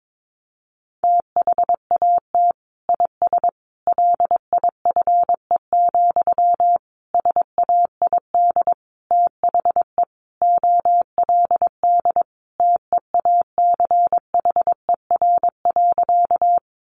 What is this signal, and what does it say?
Morse code, heard as a keyed tone, about 22 wpm, 700 Hz. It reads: THAT IS LIFE, SAID THE OLD TEACHER.
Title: Sentences from Top 900 Words 22wpm